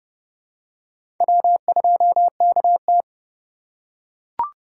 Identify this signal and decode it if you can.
Morse code, heard as a keyed tone, about 30 wpm, 700 Hz. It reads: W2KT
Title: USA Call Signs - 4 Characters - 3x Character Spacing - 30wpm